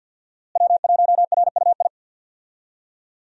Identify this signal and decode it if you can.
Morse code, heard as a keyed tone, about 52 wpm, 700 Hz. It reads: W1RUI
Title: USA Call Signs - 5 Characters - Repeated 2x - 50wpm